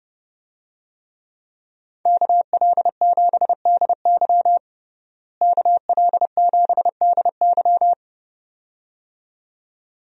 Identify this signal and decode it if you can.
Morse code, heard as a keyed tone, about 32 wpm, 700 Hz. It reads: KL7DY KL7DY
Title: DXCC-100 Call Signs - 5 Characters - Repeated 2x - 30wpm